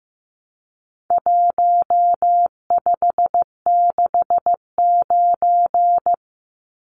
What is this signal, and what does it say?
Morse code, heard as a keyed tone, about 15 wpm, 700 Hz. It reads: 1569
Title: Numbers - Contest Style - 4-Digit 15wpm